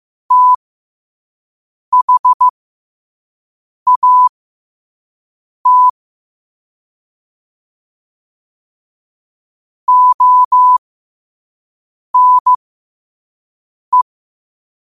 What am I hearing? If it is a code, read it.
Morse code, heard as a keyed tone, about 15 wpm, spaced as for 5 wpm, 1000 Hz. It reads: THAT ONE
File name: qotdfiles / 5wpm